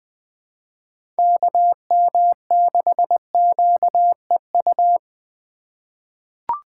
Morse code, heard as a keyed tone, about 20 wpm, 700 Hz. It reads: KM6QEU
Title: USA Call Signs - 6 Characters - 2x Character Spacing - 20wpm